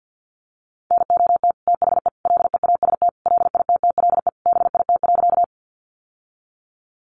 Morse code, heard as a keyed tone, about 50 wpm, 700 Hz. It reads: NOT THE LEAST LITTLE BIT.
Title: Sentences from Top 900 Words 50wpm